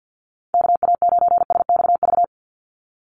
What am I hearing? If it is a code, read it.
Morse code, heard as a keyed tone, about 50 wpm, 700 Hz. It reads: KA9SXV
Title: Call Signs - 50wpm